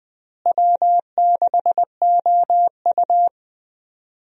Morse code, heard as a keyed tone, about 20 wpm, 700 Hz. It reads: W6OU
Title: USA Call Signs - 4 Characters - Repeated 3x - 20wpm